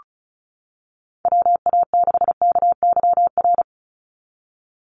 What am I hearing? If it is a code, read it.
Morse code, heard as a keyed tone, about 34 wpm, 700 Hz. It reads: WA6KYR
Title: NA SOTA Chaser Callsigns 35wpm